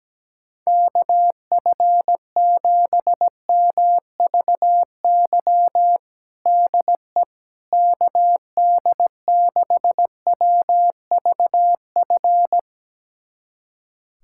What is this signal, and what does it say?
Morse code, heard as a keyed tone, about 17 wpm, 700 Hz. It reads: KF7MVY DE KD6WVF
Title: USA Callsign DE Callsign - 6 Characters - 17wpm